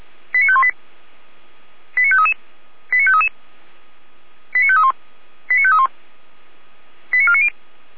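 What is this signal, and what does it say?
Five-tone selective call (ZVEI), heard as quick runs of five tones, tones of about 70 ms. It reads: 89518 89533 89533 89531 89531 89490
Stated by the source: ZVEI five-tone selective call 89519, received by SDR.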